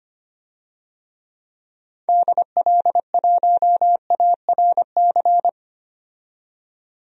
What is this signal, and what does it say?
Morse code, heard as a keyed tone, about 25 wpm, 700 Hz. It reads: DL1ARC
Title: DXCC-100 Call Signs - 6 Characters - 25wpm